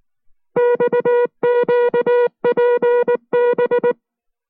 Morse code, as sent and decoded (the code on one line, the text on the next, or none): -..- --.- .--. -...
XQPB